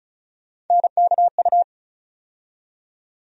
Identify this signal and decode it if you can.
Morse code, heard as a keyed tone, about 36 wpm, 700 Hz. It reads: NKU
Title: USA Call Sign Postfixes - 3 Letters 35wpm